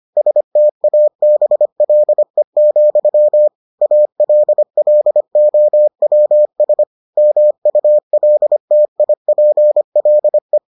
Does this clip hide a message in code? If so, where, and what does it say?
Yes: Morse code from 0.2 to 10.6 s, heard as a keyed tone, about 25 wpm, 600 Hz. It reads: STABLE, ALLOWS MULTIPLE